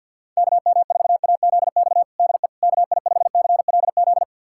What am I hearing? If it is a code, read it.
Morse code, heard as a keyed tone, about 52 wpm, 700 Hz. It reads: KM4AZX DE KI5CLZ
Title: USA Callsign DE Callsign - 6 Characters - 50wpm